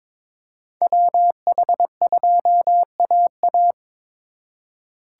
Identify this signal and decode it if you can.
Morse code, heard as a keyed tone, about 22 wpm, 700 Hz. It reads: WH2AA
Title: USA Call Signs - 5 Characters - 3x Character Spacing - 22wpm